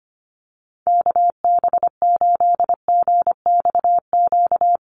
Morse code, heard as a keyed tone, about 25 wpm, 700 Hz. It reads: KB8GXQ
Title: USA Call Signs Repeated 2x 25wpm